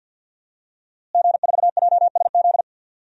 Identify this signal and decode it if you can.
Morse code, heard as a keyed tone, about 48 wpm, 700 Hz. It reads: G4JSZ
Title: DXCC-100 Call Signs - 5 Characters - Repeated 3x - 50wpm